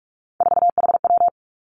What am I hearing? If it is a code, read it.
Morse code, heard as a keyed tone, about 46 wpm, 700 Hz. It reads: 4HW
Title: Three Challenging Letter-Number Combinations 45wpm